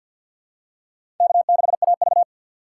Morse code, heard as a keyed tone, about 52 wpm, 700 Hz. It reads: K6AV
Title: USA Call Signs - 4 Characters - 3x Character Spacing - 50wpm